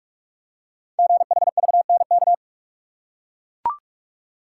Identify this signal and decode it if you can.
Morse code, heard as a keyed tone, about 44 wpm, 700 Hz. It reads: GHVNK
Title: Code Groups - 5 Letters - 45wpm